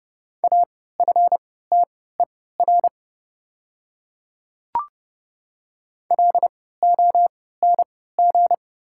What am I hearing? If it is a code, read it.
Morse code, heard as a keyed tone, about 32 wpm, spaced as for 17 wpm, 700 Hz. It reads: AFTER LONG
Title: 3x Character Spacing - Top 200 Words - 30wpm